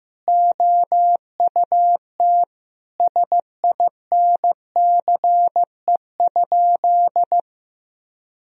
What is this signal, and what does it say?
Morse code, heard as a keyed tone, about 15 wpm, 700 Hz. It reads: OUT SINCE?
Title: Sentences from Top 1000 Words - Review Part 1 - 15wpm